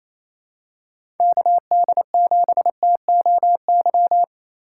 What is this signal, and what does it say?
Morse code, heard as a keyed tone, about 28 wpm, 700 Hz. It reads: KD7TOY
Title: USA Call Signs - 4x Character Spacing - 28wpm